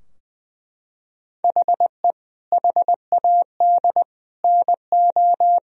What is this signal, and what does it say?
Morse code, heard as a keyed tone, about 19 wpm, 700 Hz. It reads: HE HAD NO
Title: Sentences from Top 1000 Words - Review Part 3 - 20wpm